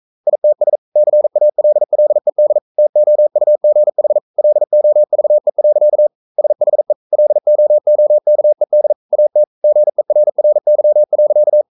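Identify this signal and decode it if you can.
Morse code, heard as a keyed tone, about 42 wpm, 600 Hz. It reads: ITS CALLED TOUGH LOVE. SHE LOOKED AT GERRY.
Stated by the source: Morse code sending of an English novel at 42 wpm.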